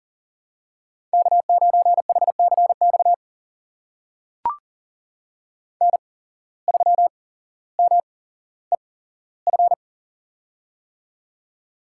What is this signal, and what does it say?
Morse code, heard as a keyed tone, about 40 wpm, spaced as for 17 wpm, 700 Hz. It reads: K9HCX N3MEF
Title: USA Call Signs - 5 Characters - 8x Character Spacing - 40wpm